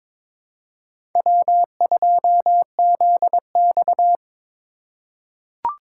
Morse code, heard as a keyed tone, about 22 wpm, 700 Hz. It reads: W2ZX
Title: USA Call Signs - 4x Character Spacing - 22wpm